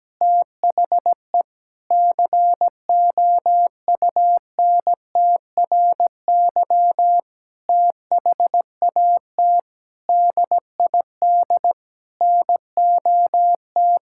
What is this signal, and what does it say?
Morse code, heard as a keyed tone, about 17 wpm, 700 Hz. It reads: THE COUNTRY THAT DID NOT
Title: Sentences from Top 1600 Words 17wpm